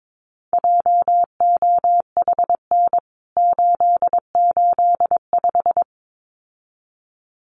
Morse code, heard as a keyed tone, about 22 wpm, 700 Hz. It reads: JOHN 885
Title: CWT Exchange 22wpm